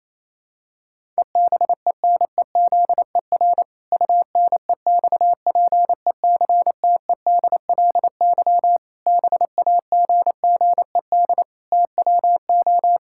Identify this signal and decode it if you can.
Morse code, heard as a keyed tone, about 28 wpm, 700 Hz. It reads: EBENEZER UNEXPECTEDLY BAGGED TWO
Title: Pangram Challenge v2 - 28wpm